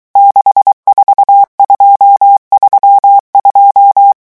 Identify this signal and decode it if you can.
Morse code, heard as a keyed tone, about 23 wpm, 800 Hz. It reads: 64232
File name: morse mp3 / revision11 chiffres vitesse 22